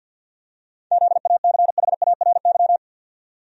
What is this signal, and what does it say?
Morse code, heard as a keyed tone, about 52 wpm, 700 Hz. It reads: ZACHARY
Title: USA Common Names - 50wpm